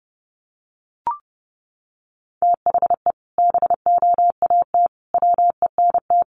Morse code, heard as a keyed tone, about 30 wpm, 700 Hz. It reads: THE BOAT WENT
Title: Sentences from Top 1100 Words 30wpm